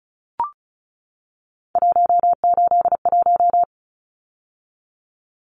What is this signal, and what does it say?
Morse code, heard as a keyed tone, about 34 wpm, 700 Hz. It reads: 181
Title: Numbers - 3-Digit 35wpm